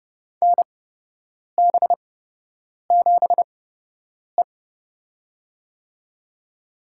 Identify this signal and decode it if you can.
Morse code, heard as a keyed tone, about 31 wpm, spaced as for 8 wpm, 700 Hz. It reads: NB7E
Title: USA Call Signs - 4 Characters - 8x Character Spacing - 30wpm